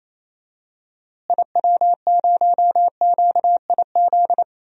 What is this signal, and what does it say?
Morse code, heard as a keyed tone, about 28 wpm, 700 Hz. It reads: IW0QIZ
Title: DXCC-100 Call Signs - 6 Characters - Repeated 3x - 28wpm